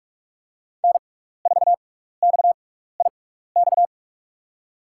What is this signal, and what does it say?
Morse code, heard as a keyed tone, about 44 wpm, spaced as for 15 wpm, 700 Hz. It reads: N4XIX
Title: USA Call Signs - 5 Characters - 6x Character Spacing - 45wpm